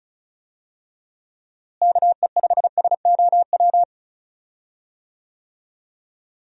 Morse code, heard as a keyed tone, about 36 wpm, 700 Hz. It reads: KE5SOW
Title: USA Call Signs - 6 Characters - Repeated 3x - 35wpm